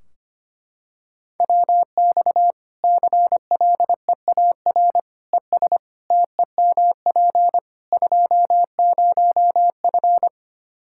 Morse code, heard as a keyed tone, about 25 wpm, 700 Hz. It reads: WX CLEAR ES TEMP 20F
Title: Practice QSO - Exchange 2 - Info - 25wpm